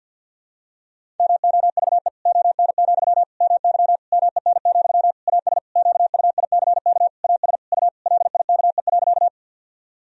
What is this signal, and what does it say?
Morse code, heard as a keyed tone, about 53 wpm, 700 Hz. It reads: MOVE ON, MY MEN, AS QUICK AS U LIKE.
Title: Sentences from Top 500 Words and Abbreviations 50wpm